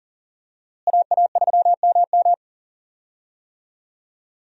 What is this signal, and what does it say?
Morse code, heard as a keyed tone, about 40 wpm, 700 Hz. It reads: AA3MM
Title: USA Call Signs - 5 Characters - Repeated 3x - 40wpm